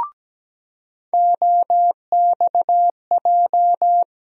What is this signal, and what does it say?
Morse code, heard as a keyed tone, about 17 wpm, 700 Hz. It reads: OXJ
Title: USA Call Sign Postfixes - 3 Letters 17wpm